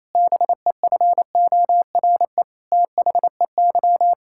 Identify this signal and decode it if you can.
Morse code, heard as a keyed tone, about 28 wpm, 700 Hz. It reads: BEFORE THEY
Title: Pangram Challenge v2 - 28wpm